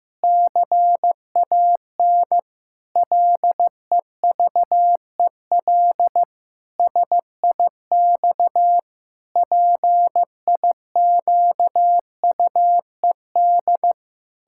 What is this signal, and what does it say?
Morse code, heard as a keyed tone, about 15 wpm, 700 Hz. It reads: CAN LEVEL SIX PIQUED
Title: Pangram Challenge v2 - 15wpm